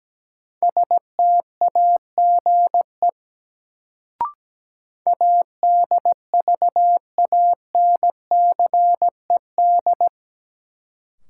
Morse code, heard as a keyed tone, about 17 wpm, 700 Hz. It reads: STAGE ADVANCED
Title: Sentences from Top 1200 Words - New Word Drill - 17wpm